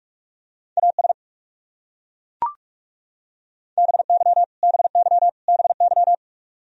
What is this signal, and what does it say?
Morse code, heard as a keyed tone, about 41 wpm, 700 Hz. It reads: AS BY BY BY
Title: Top 100 Words Repeated 45wpm